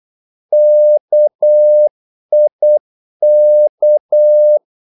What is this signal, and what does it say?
Morse code, heard as a keyed tone, about 8 wpm, 600 Hz. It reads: KIK
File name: Korte ord DK 8wpm